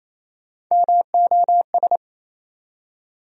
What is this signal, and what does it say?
Morse code, heard as a keyed tone, about 28 wpm, 700 Hz. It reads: MOS
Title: USA Call Sign Postfixes - 3 Letters 28wpm